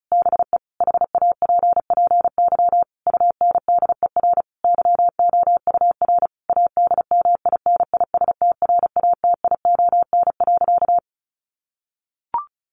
Morse code, heard as a keyed tone, about 34 wpm, 700 Hz. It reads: BE HAPPY UNDER YOUR ADMINISTRATION.
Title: Sentences from Top 1500 Words 35wpm